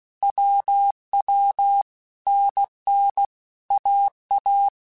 Morse code, heard as a keyed tone, about 16 wpm, 800 Hz. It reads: WW NN AA
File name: Basic Letter Groups By7Groups / AWUNDG / AWUNDG@15